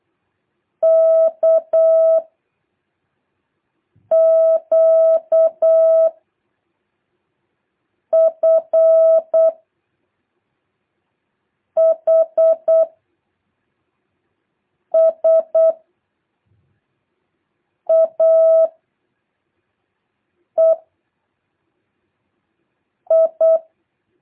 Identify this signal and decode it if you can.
Morse code, heard as a keyed tone, about 8 wpm, 660 Hz. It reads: K Q F H S A E I
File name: jh8cbh / 2023cw